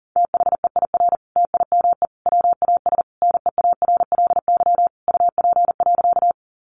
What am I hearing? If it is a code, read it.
Morse code, heard as a keyed tone, about 40 wpm, 700 Hz. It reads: THEIR TIME WAS NEARLY UP.
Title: Sentences from Top 1300 Words 40wpm